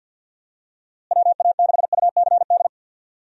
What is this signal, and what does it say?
Morse code, heard as a keyed tone, about 51 wpm, 700 Hz. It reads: WA6UCD
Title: USA Call Signs - 6 Characters - Repeated 3x - 50wpm